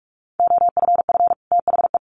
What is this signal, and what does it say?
Morse code, heard as a keyed tone, about 44 wpm, 700 Hz. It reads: OFF THE
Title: Sets of 2 Words 45wpm